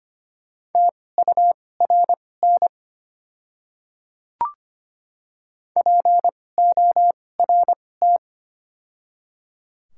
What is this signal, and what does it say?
Morse code, heard as a keyed tone, about 25 wpm, spaced as for 18 wpm, 700 Hz. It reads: TURN PORT
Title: 2x Character Spacing - Top 200 Words - 25wpm